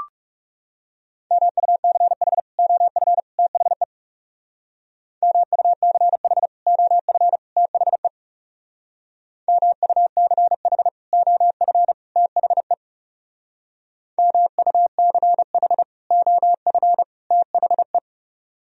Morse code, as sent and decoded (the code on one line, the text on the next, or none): -- ..- -.-. .... / --- ..-. / - .... / -- ..- -.-. .... / --- ..-. / - .... / -- ..- -.-. .... / --- ..-. / - .... / -- ..- -.-. .... / --- ..-. / - .... .
MUCH OF THE MUCH OF THE MUCH OF THE MUCH OF THE